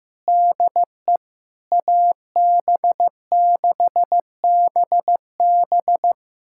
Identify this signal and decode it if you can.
Morse code, heard as a keyed tone, about 15 wpm, 700 Hz. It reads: DE AB6BB
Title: USA Callsign DE Callsign - 5 Characters - 15wpm